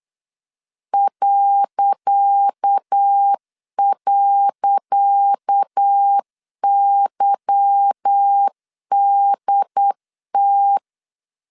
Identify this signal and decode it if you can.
Morse code, heard as a keyed tone, about 8 wpm, 800 Hz. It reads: ..YDT
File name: LEZIONI / 10 SIGNS / 10C DOT 8wpm 3649